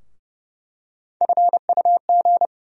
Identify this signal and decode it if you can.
Morse code, heard as a keyed tone, about 30 wpm, 700 Hz. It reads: FUG
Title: USA Call Sign Postfixes - 3 Letters 30wpm